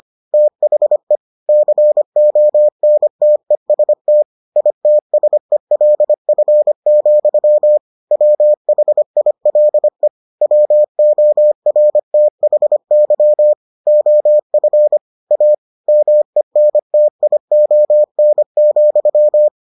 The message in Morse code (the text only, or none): THE CONTEST ITSELF, WHILE WORTHY OF A MENTION,